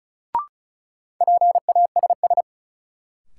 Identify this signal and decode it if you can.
Morse code, heard as a keyed tone, about 36 wpm, 700 Hz. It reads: PASS
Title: Sentences from Top 400 Words - New Word Drill - 35wpm